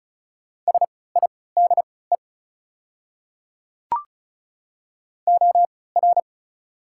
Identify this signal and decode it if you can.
Morse code, heard as a keyed tone, about 36 wpm, spaced as for 20 wpm, 690 Hz. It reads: SIDE OR
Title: 3x Character Spacing - Top 100 Words - 35wpm